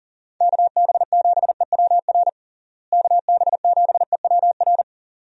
Morse code, heard as a keyed tone, about 40 wpm, 700 Hz. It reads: KB7EWR KB7EWR
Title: USA Call Signs - 6 Characters - Repeated 2x - 40wpm